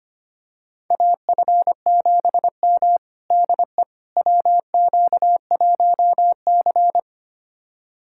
Morse code, heard as a keyed tone, about 25 wpm, 700 Hz. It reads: AF7M DE WQ1C
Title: USA Callsign DE Callsign - 4 Characters - 25wpm